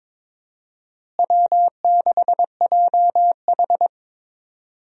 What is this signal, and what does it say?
Morse code, heard as a keyed tone, about 22 wpm, 700 Hz. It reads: W6JH
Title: USA Call Signs - 4 Characters - Repeated 3x - 22wpm